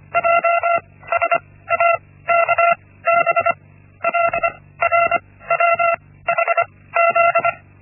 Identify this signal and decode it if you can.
Morse code, heard as a keyed tone, about 25 wpm, spaced as for 16 wpm, 680 Hz. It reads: JSAKB LRWHZ
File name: Linksmakalnis-1